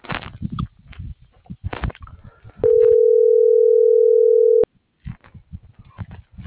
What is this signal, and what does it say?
Ringback tone (North America), 440 + 480 Hz.